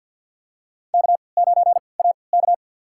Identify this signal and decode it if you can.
Morse code, heard as a keyed tone, about 46 wpm, spaced as for 27 wpm, 700 Hz. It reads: K9AK
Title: USA Call Signs - 3x Character Spacing - 50wpm